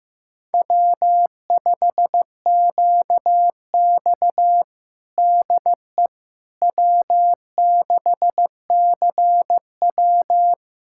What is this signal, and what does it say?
Morse code, heard as a keyed tone, about 15 wpm, 700 Hz. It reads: W5QX DE W6CW
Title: USA Callsign DE Callsign - 4 Characters - 15wpm